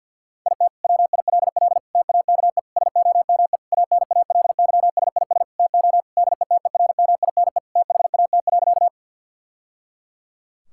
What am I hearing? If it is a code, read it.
Morse code, heard as a keyed tone, about 52 wpm, 690 Hz. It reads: IT WILL TAKE SOME ANALYSIS TO DETERMINE THAT.
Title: Sentences from Top 1000 Words - Review Part 5 - 50wpm